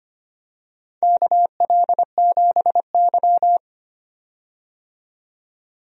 Morse code, heard as a keyed tone, about 25 wpm, 700 Hz. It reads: KL7Y
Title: DXCC-100 Call Signs - 4 Characters - 25wpm